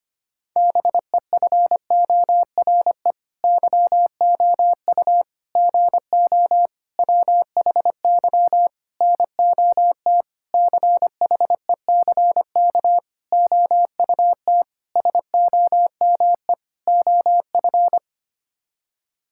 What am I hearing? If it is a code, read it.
Morse code, heard as a keyed tone, about 25 wpm, 700 Hz. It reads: BEFORE YOU GO WHY NOT CHECK OUT SOME OF
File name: Sets of 9 Words - 25wpm